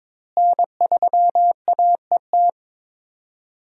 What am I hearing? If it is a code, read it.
Morse code, heard as a keyed tone, about 23 wpm, 700 Hz. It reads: N3AET
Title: USA Call Signs - 5 Characters - Repeated 2x - 22wpm